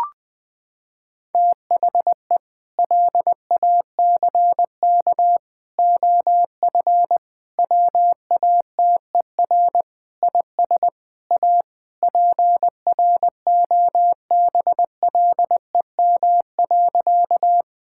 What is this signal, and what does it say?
Morse code, heard as a keyed tone, about 20 wpm, 700 Hz. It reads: THE LACK OF WATER IS A PROBLEM.
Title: Sentences with 3-Word Phrases - 20wpm